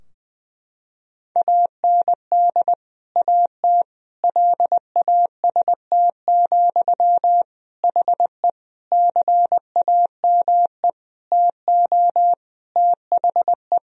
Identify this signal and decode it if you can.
Morse code, heard as a keyed tone, about 20 wpm, 700 Hz. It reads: AND AT LAST, HE CAME TO THE
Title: Sentences from Top 1000 Words - Review Part 2 - 20wpm